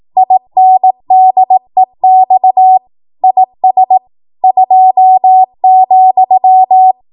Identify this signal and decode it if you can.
Morse code, heard as a keyed tone, about 18 wpm, 750 Hz. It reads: INDEX IS 2,
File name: Morse / 18WPM